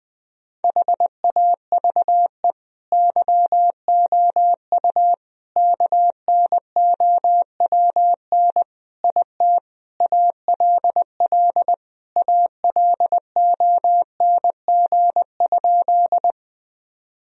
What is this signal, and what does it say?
Morse code, heard as a keyed tone, about 20 wpm, 700 Hz. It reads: HAVE YOU KNOWN IT ALL ALONG?
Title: Sentences from Top 1000 Words - Review Part 3 - 20wpm